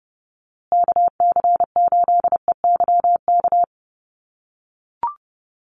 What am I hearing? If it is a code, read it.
Morse code, heard as a keyed tone, about 30 wpm, 700 Hz. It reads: KC8EYK
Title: USA Call Signs Repeated 2x 30wpm